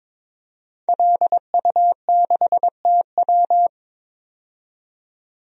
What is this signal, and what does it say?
Morse code, heard as a keyed tone, about 21 wpm, 700 Hz. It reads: LU6TW
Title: DXCC-100 Call Signs - 5 Characters - 22wpm